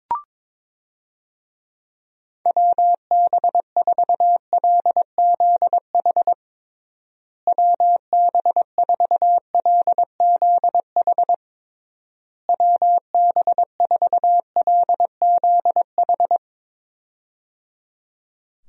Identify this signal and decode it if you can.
Morse code, heard as a keyed tone, about 22 wpm, 700 Hz. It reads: WB4LZH WB4LZH WB4LZH
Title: USA Call Signs - 6 Characters - Repeated 3x - 22wpm